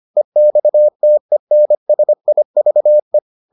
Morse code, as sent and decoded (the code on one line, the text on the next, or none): . -..- - . -. ... .. ...- .
EXTENSIVE